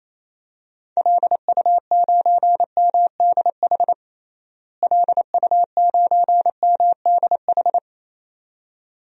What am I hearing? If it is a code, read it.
Morse code, heard as a keyed tone, about 28 wpm, 700 Hz. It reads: LU9MDH LU9MDH
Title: DXCC-100 Call Signs - 6 Characters - Repeated 2x - 28wpm